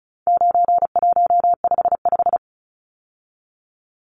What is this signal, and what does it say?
Morse code, heard as a keyed tone, about 34 wpm, 700 Hz. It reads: 9155
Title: Numbers - 4-Digit 35wpm